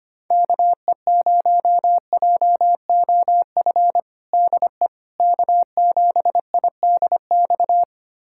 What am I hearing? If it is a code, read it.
Morse code, heard as a keyed tone, about 25 wpm, 700 Hz. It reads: KE0JOF DE K7IDX
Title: USA Callsign DE Callsign 25wpm